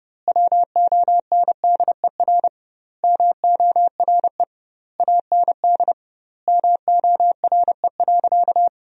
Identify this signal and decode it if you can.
Morse code, heard as a keyed tone, about 30 wpm, 700 Hz. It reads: WONDER MORE AND MORE.
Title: 2x Word-Spacing - Sentences from Top 500 Words - 30wpm